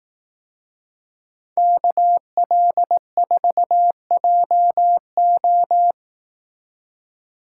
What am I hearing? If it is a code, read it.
Morse code, heard as a keyed tone, about 18 wpm, 700 Hz. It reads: KL4JO
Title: Speed Racing - USA 2x2 Call Signs 15wpm